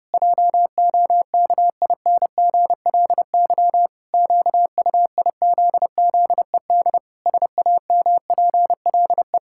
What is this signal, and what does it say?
Morse code, heard as a keyed tone, about 32 wpm, 700 Hz. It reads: JOKINGLY QUIZZED SAMPLE
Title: Pangram Challenge v2 - 30wpm